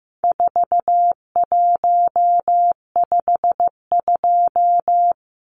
Morse code, heard as a keyed tone, about 15 wpm, 700 Hz. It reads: 4152
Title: Numbers - 4-Digit 15wpm